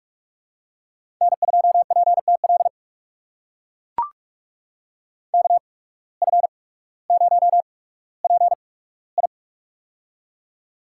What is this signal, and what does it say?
Morse code, heard as a keyed tone, about 44 wpm, spaced as for 19 wpm, 700 Hz. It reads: N2PTL KF0PI
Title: USA Call Signs - 5 Characters - 8x Character Spacing - 45wpm